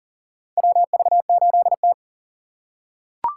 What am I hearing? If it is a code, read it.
Morse code, heard as a keyed tone, about 40 wpm, 700 Hz. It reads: WV8T
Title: USA Call Signs - 4 Characters - 3x Character Spacing - 40wpm